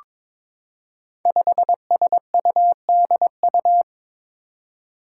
Morse code, heard as a keyed tone, about 22 wpm, 700 Hz. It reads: 5SUDU
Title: Five Challenging Letter-Number Combinations 22wpm